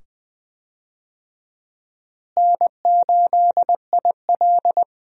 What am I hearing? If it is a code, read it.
Morse code, heard as a keyed tone, about 20 wpm, 700 Hz. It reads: N8IL
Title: USA Call Signs - 4 Characters - Repeated 2x - 20wpm